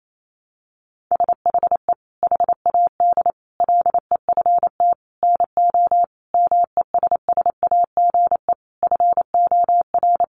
Morse code, heard as a keyed tone, about 28 wpm, 700 Hz. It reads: SHE HAD LEFT NO MESSAGE FOR
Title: Sentences from Top 1100 Words 28wpm